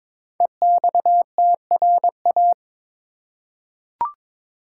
Morse code, heard as a keyed tone, about 22 wpm, 700 Hz. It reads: EXTRA
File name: 5 Letter Words - First 3 Letters Given - 22wpm